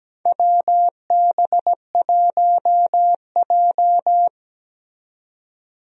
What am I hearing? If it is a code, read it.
Morse code, heard as a keyed tone, about 17 wpm, 700 Hz. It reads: WB1J
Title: USA Call Signs - 4 Characters - 17wpm